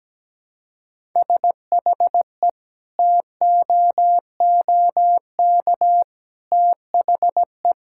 Morse code, heard as a keyed tone, about 17 wpm, 700 Hz. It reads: SHE TOOK THE
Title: Sentences from Top 1000 Words - Review Part 5 - 17wpm